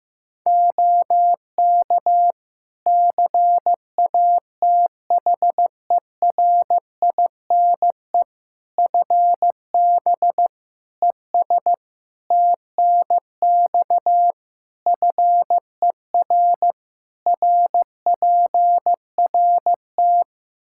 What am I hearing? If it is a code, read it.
Morse code, heard as a keyed tone, about 15 wpm, 700 Hz. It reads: OK CATHERINE FB ES TNX FER RPRT
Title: Practice QSO - Exchange 2 - Info - 15wpm